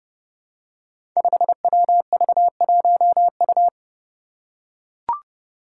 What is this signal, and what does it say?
Morse code, heard as a keyed tone, about 30 wpm, 700 Hz. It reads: 5WV1U
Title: Five Challenging Letter-Number Combinations 30wpm